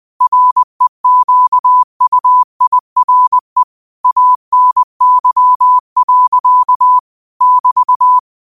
Morse code, heard as a keyed tone, about 20 wpm, 1000 Hz. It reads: REQUIRE ANY. <BT>
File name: qotdfiles / 20wpm